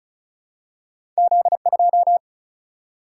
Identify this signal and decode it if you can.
Morse code, heard as a keyed tone, about 36 wpm, 700 Hz. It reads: Z2
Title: Single-Letter Single-Number 35wpm